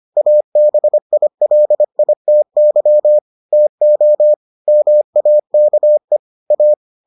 Morse code, heard as a keyed tone, about 25 wpm, 600 Hz. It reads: ABILITY TO MAKE A